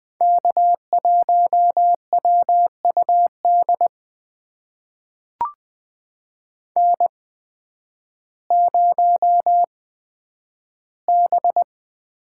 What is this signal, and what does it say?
Morse code, heard as a keyed tone, about 20 wpm, spaced as for 11 wpm, 700 Hz. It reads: K1WUD N0B
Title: USA Call Signs - 8x Character Spacing - 20wpm